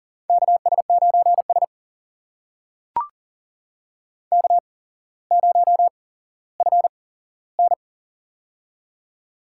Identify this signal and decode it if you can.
Morse code, heard as a keyed tone, about 40 wpm, spaced as for 17 wpm, 700 Hz. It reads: KS9S K0FN
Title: USA Call Signs - 4 Characters - 8x Character Spacing - 40wpm